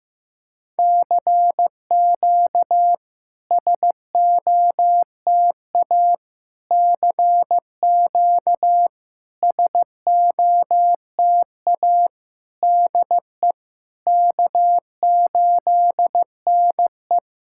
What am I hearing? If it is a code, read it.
Morse code, heard as a keyed tone, about 15 wpm, 700 Hz. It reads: CQ SOTA CQ SOTA DE K8NE
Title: SOTA QSO Practice 15wpm